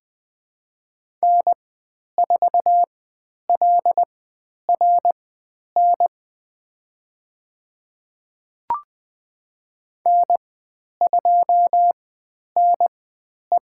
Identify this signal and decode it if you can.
Morse code, heard as a keyed tone, about 20 wpm, spaced as for 10 wpm, 700 Hz. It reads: N4LRN N2NE
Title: 109 Lesson 18g - Letter m - Call Signs 20wpm